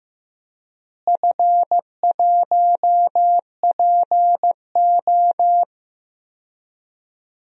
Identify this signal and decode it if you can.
Morse code, heard as a keyed tone, about 15 wpm, 700 Hz. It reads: F1PO
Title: DXCC-100 Call Signs - 4 Characters - Repeated 2x - 15wpm